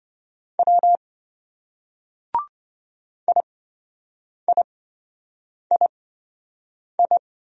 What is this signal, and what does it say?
Morse code, heard as a keyed tone, about 27 wpm, spaced as for 7 wpm, 700 Hz. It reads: W IIII